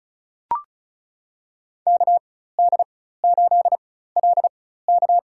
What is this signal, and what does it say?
Morse code, heard as a keyed tone, about 36 wpm, spaced as for 16 wpm, 700 Hz. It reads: KD8LK
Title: USA Call Signs - 4x Character Spacing - 35wpm